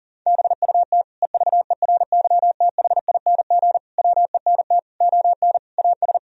Morse code, heard as a keyed tone, about 40 wpm, 700 Hz. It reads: BUT EVERYTHING WENT ON AS
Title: Sentences from Top 1000 Words - Review Part 2 - 40wpm